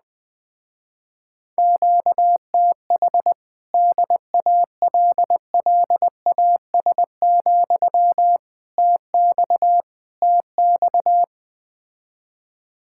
Morse code, heard as a keyed tone, about 20 wpm, 700 Hz. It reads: QTH DALLAS, TX TX
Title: Speed-Racing Common QSO Phrases - 17wpm